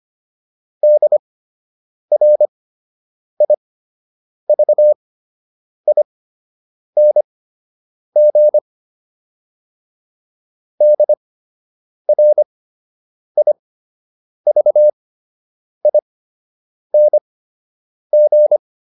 Morse code, heard as a keyed tone, about 25 wpm, spaced as for 8 wpm, 600 Hz. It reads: DRIVING DRIVING